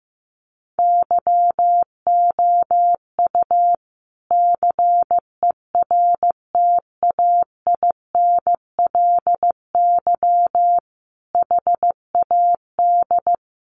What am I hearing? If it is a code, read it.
Morse code, heard as a keyed tone, about 15 wpm, 700 Hz. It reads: YOU CERTAINLY HAD
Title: Sentences from Top 1200 Words 15wpm